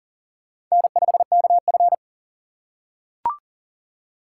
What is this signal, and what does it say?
Morse code, heard as a keyed tone, about 40 wpm, 700 Hz. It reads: N5KF
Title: USA Call Signs - 4 Characters - 3x Character Spacing - 40wpm